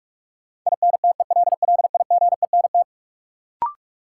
Morse code, heard as a keyed tone, about 43 wpm, 700 Hz. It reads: INTELLIGENT
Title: Sentences from Top 1000 Words - Review Part 7 - New Word Drill - 45wpm